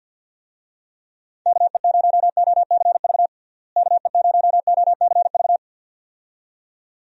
Morse code, heard as a keyed tone, about 50 wpm, 700 Hz. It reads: KE0OCV KE0OCV
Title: USA Call Signs - 6 Characters - Repeated 2x - 50wpm